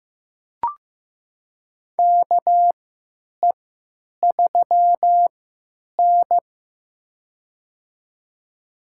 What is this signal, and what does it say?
Morse code, heard as a keyed tone, about 15 wpm, spaced as for 9 wpm, 700 Hz. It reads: KE3N
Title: USA Call Signs - 3x Character Spacing - 15wpm